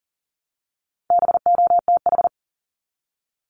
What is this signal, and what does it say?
Morse code, heard as a keyed tone, about 40 wpm, 700 Hz. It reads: BOTH
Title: Top 500 Words Repeated 40wpm